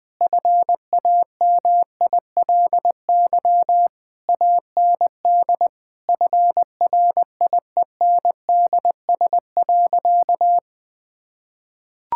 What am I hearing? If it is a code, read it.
Morse code, heard as a keyed tone, about 20 wpm, 700 Hz. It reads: FAMILY AND FRIENDS.
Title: Sentences from Top 1000 Words - Review Part 5 - 20wpm